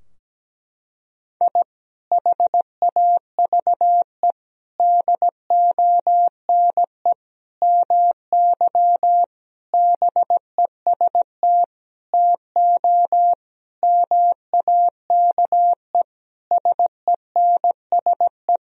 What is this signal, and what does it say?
Morse code, heard as a keyed tone, about 17 wpm, 700 Hz. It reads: I HAVE DONE MY BEST TO MAKE SENSE
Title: Sentences from Top 1000 Words - Review Part 4 - 17wpm